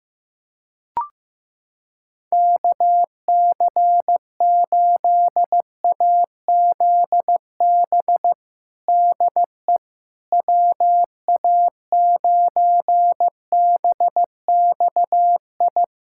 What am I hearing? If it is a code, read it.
Morse code, heard as a keyed tone, about 15 wpm, 700 Hz. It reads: KC8AZB DE WA9BXI
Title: K1USN SST QSO Practice 15wpm